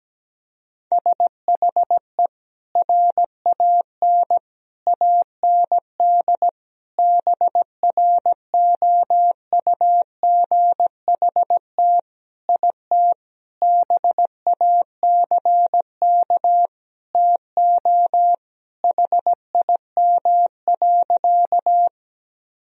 Morse code, heard as a keyed tone, about 17 wpm, 700 Hz. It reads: SHE RAN AND BROUGHT IT BACK TO HIM.
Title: Sentences from Top 1000 Words - Review Part 3 - 17wpm